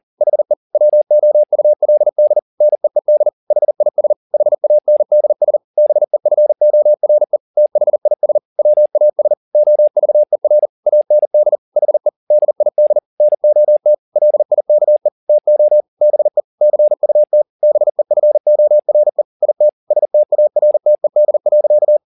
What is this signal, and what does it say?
Morse code, heard as a keyed tone, about 40 wpm, 600 Hz. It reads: HE WOULD NEED HIS HANDS BEFORE THIS WAS OVER AND HE DID NOT LIKE TO BE CUT BEFORE IT STARTED.